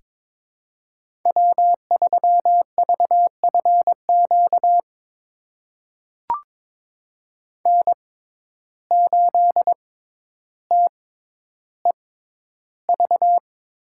Morse code, heard as a keyed tone, about 22 wpm, spaced as for 11 wpm, 700 Hz. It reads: W3VFQ N8TEV
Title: USA Call Signs - 5 Characters - 6x Character Spacing - 22wpm